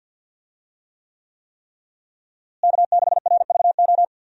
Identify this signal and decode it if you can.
Morse code, heard as a keyed tone, about 50 wpm, 700 Hz. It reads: K6RVO
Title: USA Call Signs - 5 Characters - Repeated 3x - 50wpm